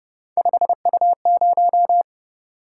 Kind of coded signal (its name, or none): Morse code